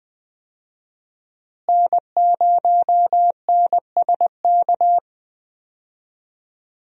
Morse code, heard as a keyed tone, about 20 wpm, 700 Hz. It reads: N0NSK
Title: USA Call Signs - 5 Characters - Repeated 2x - 20wpm